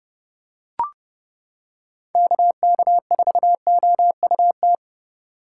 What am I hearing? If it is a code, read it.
Morse code, heard as a keyed tone, about 30 wpm, 700 Hz. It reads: KK4OUT